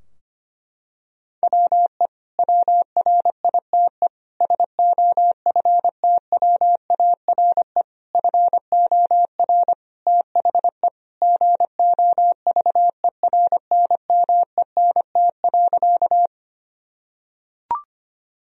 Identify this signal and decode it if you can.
Morse code, heard as a keyed tone, about 25 wpm, 700 Hz. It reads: WE WRITE SOFTWARE FOR THE GOVERNMENT.
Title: Sentences from Top 1000 Words - Review Part 3 - 25wpm